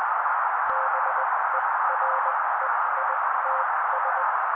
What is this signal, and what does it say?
Morse code, heard as a keyed tone, about 20 wpm, spaced as for 16 wpm, 530 Hz. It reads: BEREITS
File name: daily / 16wpm noise